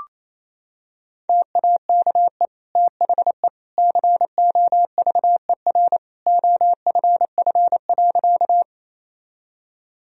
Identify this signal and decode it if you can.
Morse code, heard as a keyed tone, about 28 wpm, 700 Hz. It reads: TAKE THE COVER OFF.
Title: Sentences from Top 1000 Words - Review Part 1 - 28wpm